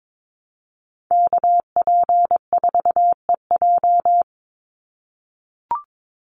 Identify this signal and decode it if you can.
Morse code, heard as a keyed tone, about 22 wpm, 700 Hz. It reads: KP4EJ
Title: Call Signs - 22wpm